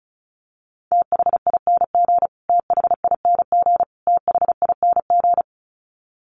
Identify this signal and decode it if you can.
Morse code, heard as a keyed tone, about 34 wpm, 700 Hz. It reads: THING THING THING
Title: Top 500 Words Repeated 35wpm